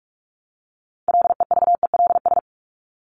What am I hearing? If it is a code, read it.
Morse code, heard as a keyed tone, about 44 wpm, 700 Hz. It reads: LEVELS
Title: ES:S Suffix - Top 100 Words with Base Words 45wpm